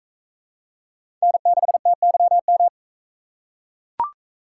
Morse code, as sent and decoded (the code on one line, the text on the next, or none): -. -.... - -.-- --
N6TYM